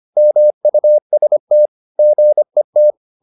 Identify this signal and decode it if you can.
Morse code, heard as a keyed tone, about 25 wpm, 600 Hz. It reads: MUST GET
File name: the old man and the sea part 1 25wpm